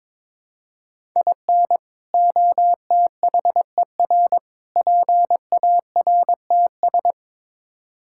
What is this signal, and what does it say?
Morse code, heard as a keyed tone, about 21 wpm, 700 Hz. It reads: IN OTHER PARTS
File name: practice / 22wpm / Build A Phrase - 5-Word Phrases - 22wpm